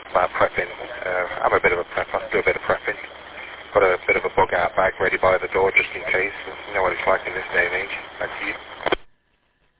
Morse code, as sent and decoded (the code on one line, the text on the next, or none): none